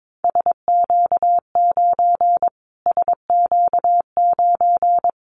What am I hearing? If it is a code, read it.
Morse code, heard as a keyed tone, about 22 wpm, 700 Hz. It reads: SQ9 SQ9
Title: Callsign Prefixes for Top 25 DXCC - Repeated 3x - 22wpm